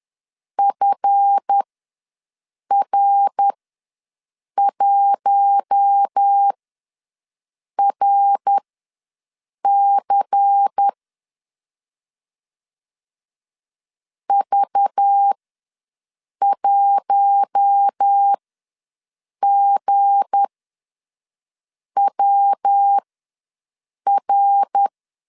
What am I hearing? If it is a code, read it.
Morse code, heard as a keyed tone, about 10 wpm, spaced as for 5 wpm, 800 Hz. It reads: FR1RC V1GWR
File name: LEZIONI / 7 GF9Z / 7 GF ws 10wpm 3918